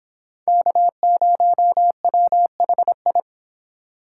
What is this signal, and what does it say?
Morse code, heard as a keyed tone, about 26 wpm, 700 Hz. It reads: K0WHI